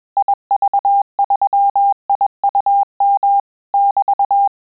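Morse code, heard as a keyed tone, about 21 wpm, 800 Hz. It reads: IV3IUM <BT>
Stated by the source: Morse QSO practice at 20 words per minute, in Italian and English.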